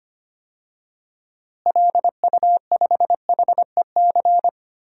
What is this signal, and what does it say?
Morse code, heard as a keyed tone, about 25 wpm, 700 Hz. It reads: LU5HEC
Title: DXCC-100 Call Signs - 6 Characters - Repeated 2x - 25wpm